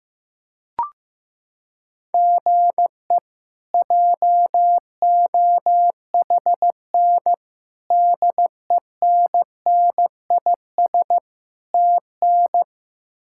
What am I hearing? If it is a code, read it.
Morse code, heard as a keyed tone, about 16 wpm, 700 Hz. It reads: GE JOHN DENNIS TN
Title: K1USN SST QSO Practice - 15wpm